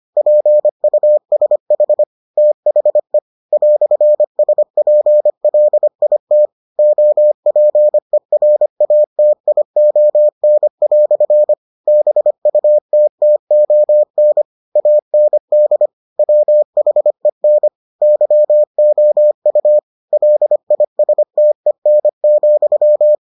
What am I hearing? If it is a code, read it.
Morse code, heard as a keyed tone, about 25 wpm, 600 Hz. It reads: PUSH THE "SPLIT OPERATION" BUTTON AND WHEN YOU LISTEN,